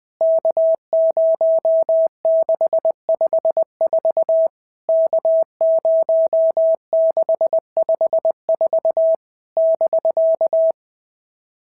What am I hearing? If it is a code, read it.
Morse code, heard as a keyed tone, about 20 wpm, 650 Hz. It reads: K0654 K0654 <BK>
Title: POTA QSO Practice - 20wpm